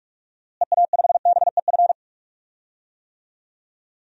Morse code, heard as a keyed tone, about 46 wpm, 700 Hz. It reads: EA5BEF
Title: DXCC-100 Call Signs - 6 Characters - Repeated 2x - 45wpm